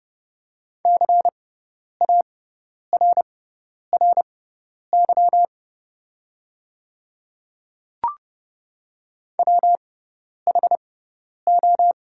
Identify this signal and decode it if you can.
Morse code, heard as a keyed tone, about 30 wpm, spaced as for 10 wpm, 690 Hz. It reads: CARRY WHO
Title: 6x Character Spacing - Top 300 Words - 30wpm